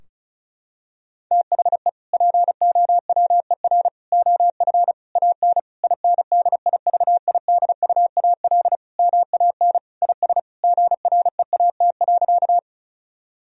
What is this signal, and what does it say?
Morse code, heard as a keyed tone, about 34 wpm, 700 Hz. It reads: THE POWER OF AN INDIVIDUAL MAN IS GREAT.
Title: Sentences from Top 1000 Words - Review Part 4 - 35wpm